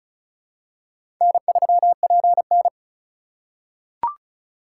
Morse code, heard as a keyed tone, about 36 wpm, 700 Hz. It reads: N3PN
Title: USA Call Signs - 4 Characters - 2x Character Spacing - 35wpm